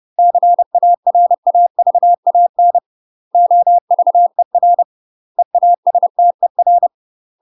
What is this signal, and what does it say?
Morse code, heard as a keyed tone, about 30 wpm, 700 Hz. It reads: CARAVAN OVER EASTER
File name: UK MASTER QSOs TRAINING / 30WPM / UK MASTER QSOs 0017